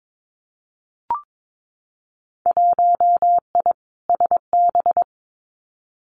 Morse code, heard as a keyed tone, about 22 wpm, 700 Hz. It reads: 1I SB
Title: Winter Field Day Exchanges 22wpm